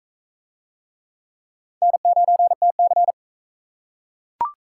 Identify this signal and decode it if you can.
Morse code, heard as a keyed tone, about 41 wpm, 700 Hz. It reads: N9TC